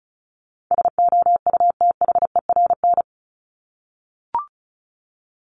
Morse code, heard as a keyed tone, about 34 wpm, 700 Hz. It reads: SOUTHERN
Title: Top 1751-2000 Words 35wpm